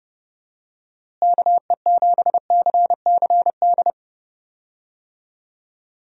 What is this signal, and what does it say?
Morse code, heard as a keyed tone, about 30 wpm, 700 Hz. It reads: KE7CCD
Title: USA Call Signs - 6 Characters - Repeated 2x - 30wpm